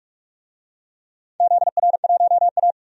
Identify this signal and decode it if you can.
Morse code, heard as a keyed tone, about 46 wpm, 700 Hz. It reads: ZR1A
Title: DXCC-100 Call Signs - 4 Characters - 45wpm